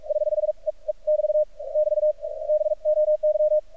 Morse code, heard as a keyed tone, about 46 wpm, spaced as for 36 wpm, 610 Hz. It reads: <HH>E E<BT> <BT> BOY
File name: archived / CRSnet